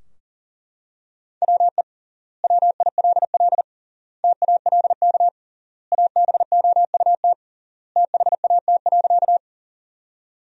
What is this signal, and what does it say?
Morse code, heard as a keyed tone, about 40 wpm, 700 Hz. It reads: WE WILL TALK ABOUT THAT.
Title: 3x Word-Spacing - Sentences from Top 400 Words - 40wpm